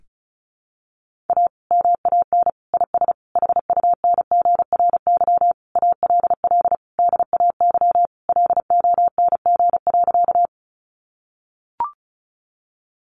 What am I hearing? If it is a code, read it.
Morse code, heard as a keyed tone, about 34 wpm, 700 Hz. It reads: A MAN IS HUNGRY ALL DAY LONG.
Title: Sentences from Top 1400 Words 35wpm